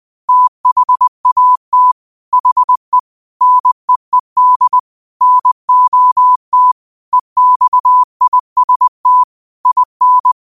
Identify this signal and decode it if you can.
Morse code, heard as a keyed tone, about 20 wpm, 1000 Hz. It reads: THAT HE NEED NOT EXIST IN
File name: qotdfiles / 20wpm